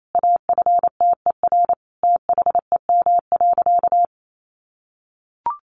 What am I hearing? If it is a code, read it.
Morse code, heard as a keyed tone, about 28 wpm, 700 Hz. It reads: AFTER THEM.
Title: A Sound Of Thunder 28wpm